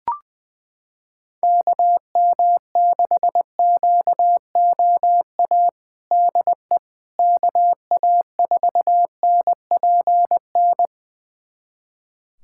Morse code, heard as a keyed tone, about 20 wpm, 700 Hz. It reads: KM6QOA DE KA4NPN
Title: USA Callsign DE Callsign - 6 Characters - 20wpm